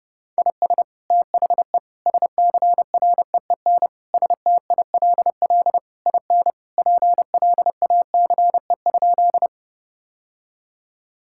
Morse code, heard as a keyed tone, about 30 wpm, 700 Hz. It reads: IS THE SCREEN STILL IN PLACE?
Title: Sentences from Top 1000 Words - Review Part 6 - 30wpm